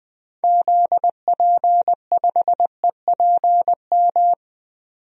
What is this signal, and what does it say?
Morse code, heard as a keyed tone, about 20 wpm, 700 Hz. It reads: ZP5EPM
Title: DXCC-100 Call Signs - 6 Characters - 20wpm